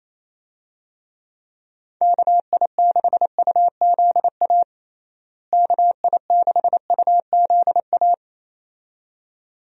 Morse code, heard as a keyed tone, about 28 wpm, 700 Hz. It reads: KI6UZA KI6UZA
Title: USA Call Signs - 6 Characters - Repeated 2x - 28wpm